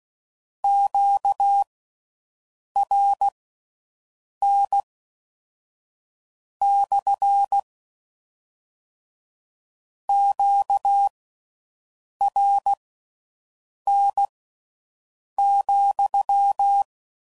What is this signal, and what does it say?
Morse code, heard as a keyed tone, about 16 wpm, spaced as for 6 wpm, 800 Hz. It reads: QRN / QRN,